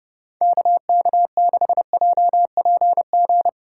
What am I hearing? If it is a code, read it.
Morse code, heard as a keyed tone, about 29 wpm, 690 Hz. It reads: KK6JPG